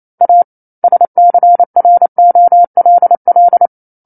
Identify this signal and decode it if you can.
Morse code, heard as a keyed tone, about 28 wpm, 700 Hz. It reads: A SCROLL
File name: 09-Chapter-09 28 WPM